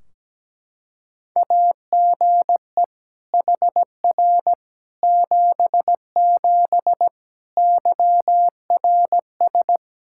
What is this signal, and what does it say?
Morse code, heard as a keyed tone, about 17 wpm, 700 Hz. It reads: AGE HR 77 YRS
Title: Practice QSO - Exchange 3 - More Info - 17wpm